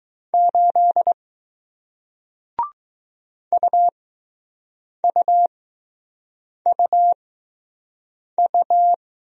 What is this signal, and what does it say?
Morse code, heard as a keyed tone, about 20 wpm, spaced as for 6 wpm, 700 Hz. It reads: 8 UUUU